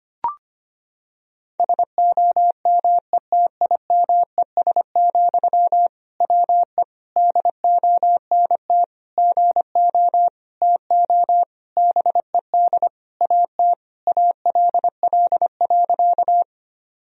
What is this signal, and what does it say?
Morse code, heard as a keyed tone, about 25 wpm, 700 Hz. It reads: SOMETIMES, WE DONT GO TO BED AT ALL.
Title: Sentences from Top 1000 Words - Review Part 3 - 25wpm